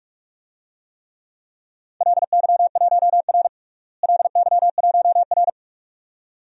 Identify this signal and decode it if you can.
Morse code, heard as a keyed tone, about 43 wpm, 700 Hz. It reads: LY1R LY1R
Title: DXCC-100 Call Signs - 4 Characters - Repeated 2x - 45wpm